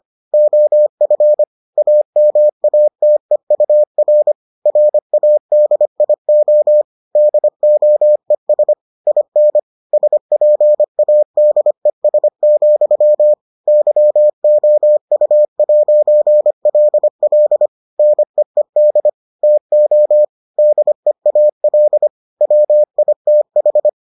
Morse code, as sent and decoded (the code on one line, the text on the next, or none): --- ..-. / .- -- .- - . ..- .-. / .-. .- -.. .. --- / -.. --- . ... / .. -. / ... .--. .- -.. ... --..-- / -.-- --- ..- .----. .-.. .-.. / -. . -.. / - --- / -.. .- .-.. / .-- .. - ....
OF AMATEUR RADIO DOES IN SPADES, YOU'LL NEED TO DEAL WITH